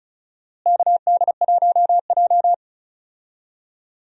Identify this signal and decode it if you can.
Morse code, heard as a keyed tone, about 34 wpm, 700 Hz. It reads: KD1J
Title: USA Call Signs - 4 Characters - Repeated 2x - 35wpm